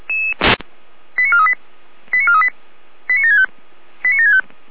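Five-tone selective call (ZVEI), heard as quick runs of five tones, tones of about 70 ms. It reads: 89438 89438 89765 89765
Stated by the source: A ZVEI five-tone selective call, code 89560, received with an SDR.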